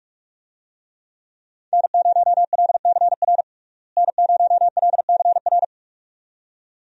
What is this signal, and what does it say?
Morse code, heard as a keyed tone, about 43 wpm, 700 Hz. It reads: N0LCR N0LCR
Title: USA Call Signs - 5 Characters - Repeated 2x - 45wpm